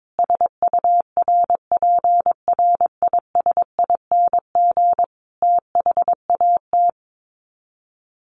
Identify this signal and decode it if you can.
Morse code, heard as a keyed tone, about 22 wpm, 700 Hz. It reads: SURPRISING THAT
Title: Sets of 5 Words 22wpm